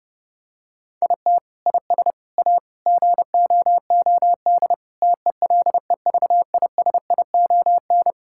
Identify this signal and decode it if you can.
Morse code, heard as a keyed tone, about 30 wpm, 700 Hz. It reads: IT IS A GOOD TELEVISION
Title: Sentences from Top 1000 Words - Review Part 6 - 30wpm